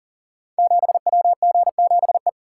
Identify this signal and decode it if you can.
Morse code, heard as a keyed tone, about 40 wpm, 700 Hz. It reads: 7WG7E
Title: Code Groups - 5 Letter:Numbers - 40wpm